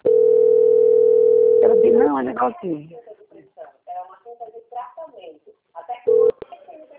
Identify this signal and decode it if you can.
Ringback tone (North America), 440 + 480 Hz, with speech.
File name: out- - - - - .99692